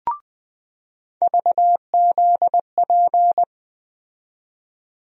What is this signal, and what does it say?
Morse code, heard as a keyed tone, about 20 wpm, 700 Hz. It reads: VZP
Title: USA Call Sign Postfixes - 3 Letters 20wpm